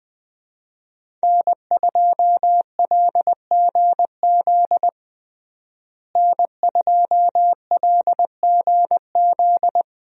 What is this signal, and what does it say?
Morse code, heard as a keyed tone, about 21 wpm, 700 Hz. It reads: N2LGZ N2LGZ
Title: USA Call Signs - 5 Characters - Repeated 3x - 20wpm